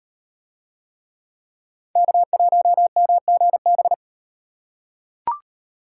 Morse code, heard as a keyed tone, about 38 wpm, 700 Hz. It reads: K1MGB